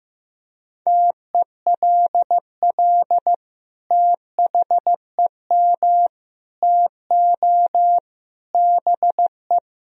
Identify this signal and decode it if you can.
Morse code, heard as a keyed tone, about 16 wpm, 700 Hz. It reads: TELL THEM TO BE
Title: Sentences from Top 1000 Words - Review Part 3 - 15wpm